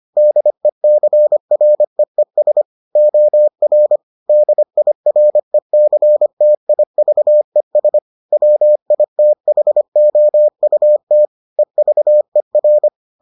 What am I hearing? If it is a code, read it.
Morse code, heard as a keyed tone, about 25 wpm, 600 Hz. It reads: DECREES OR DIRECTIVES WITHOUT EVER